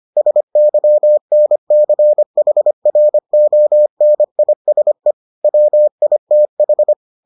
Morse code, heard as a keyed tone, about 25 wpm, 600 Hz. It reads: SYNCHRONISE WITH